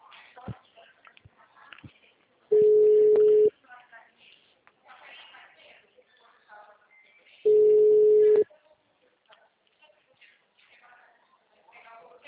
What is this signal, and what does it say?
Ringback tone (Europe, 425 Hz), with speech.